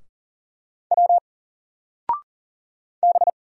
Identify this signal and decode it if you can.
Morse code, heard as a keyed tone, about 40 wpm, 700 Hz. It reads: W B
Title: Single Character (Letter:Number:Punctuation:Prosign) 40wpm